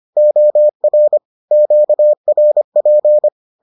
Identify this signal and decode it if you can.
Morse code, heard as a keyed tone, about 25 wpm, 600 Hz. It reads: OR QRP